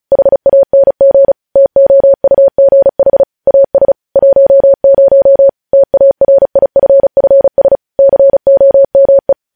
Morse code, heard as a keyed tone, about 36 wpm, 550 Hz. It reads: HANG TOUGH AS 10 TARIFFS COME
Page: News Headlines in Morse Code at 35 WPM